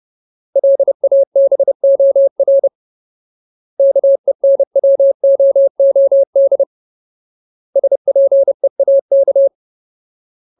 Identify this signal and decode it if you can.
Morse code, heard as a keyed tone, about 30 wpm, 550 Hz. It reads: LABOR KENWOOD SPEAK
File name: Ultimate Word List / 30WPM Ultimate